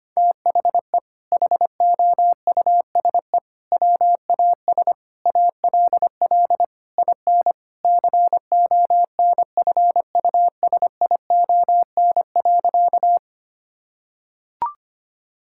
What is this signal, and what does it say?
Morse code, heard as a keyed tone, about 25 wpm, 700 Hz. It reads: THE HOUSE WAS ALL IN CONFUSION.
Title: Sentences from Top 1600 Words 25wpm